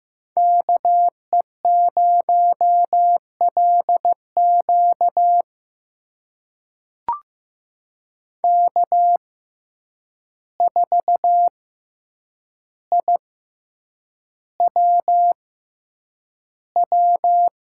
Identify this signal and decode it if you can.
Morse code, heard as a keyed tone, about 15 wpm, spaced as for 8 wpm, 700 Hz. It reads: KE0LQ K4IWW
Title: USA Call Signs - 5 Characters - 6x Character Spacing - 15wpm